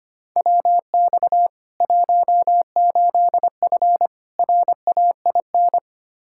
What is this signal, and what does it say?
Morse code, heard as a keyed tone, about 25 wpm, 700 Hz. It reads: WX 18F RAIN
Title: Practice QSO - Exchange 2 - Info - 25wpm